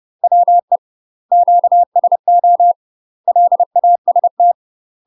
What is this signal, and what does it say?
Morse code, heard as a keyed tone, about 30 wpm, 700 Hz. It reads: WE QSO LAST
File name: UK MASTER QSOs TRAINING / 30WPM / UK MASTER QSOs 0002